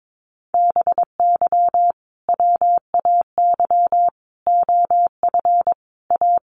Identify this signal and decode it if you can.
Morse code, heard as a keyed tone, about 22 wpm, 700 Hz. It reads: BY WAY OF A
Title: Sentences from Top 1500 Words 22wpm